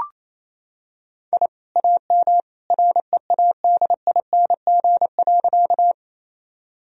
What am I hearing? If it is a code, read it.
Morse code, heard as a keyed tone, about 28 wpm, 700 Hz. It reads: I AM READING.
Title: Sentences from Top 1000 Words - Review Part 3 - 28wpm